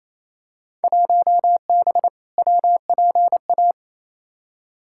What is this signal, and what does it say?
Morse code, heard as a keyed tone, about 28 wpm, 700 Hz. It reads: 1B WPA
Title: Field Day Exchanges Encore - 28wpm